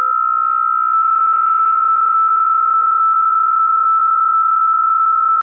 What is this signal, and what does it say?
PSK63, about 62.5 baud.